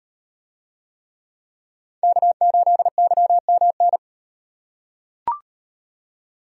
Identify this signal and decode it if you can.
Morse code, heard as a keyed tone, about 38 wpm, 700 Hz. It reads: K8YMN